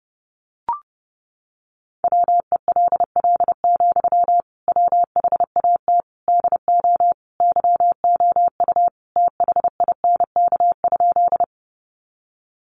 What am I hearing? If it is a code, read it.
Morse code, heard as a keyed tone, about 30 wpm, 700 Hz. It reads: WELL, WHAT DO YOU THINK?
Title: Sentences from Top 200 Words 30wpm